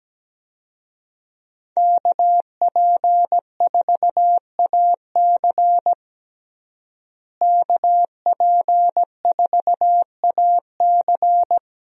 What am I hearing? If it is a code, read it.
Morse code, heard as a keyed tone, about 17 wpm, 700 Hz. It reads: KP4AC KP4AC
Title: USA Call Signs - 5 Characters - Repeated 3x - 17wpm